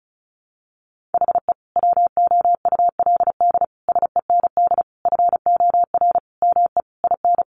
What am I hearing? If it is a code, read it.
Morse code, heard as a keyed tone, about 34 wpm, 700 Hz. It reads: HE WOULD SEND FOR ME IN
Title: Sentences from Top 1500 Words 35wpm